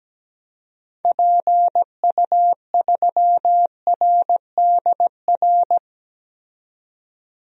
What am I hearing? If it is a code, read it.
Morse code, heard as a keyed tone, about 17 wpm, 700 Hz. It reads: PU3RDR
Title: DXCC-100 Call Signs - 6 Characters - 17wpm